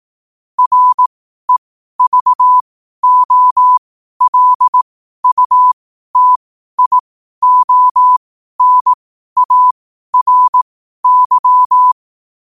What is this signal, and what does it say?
Morse code, heard as a keyed tone, about 18 wpm, spaced as for 12 wpm, 1000 Hz. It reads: REVOLUTIONARY
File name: qotdfiles / 12wpm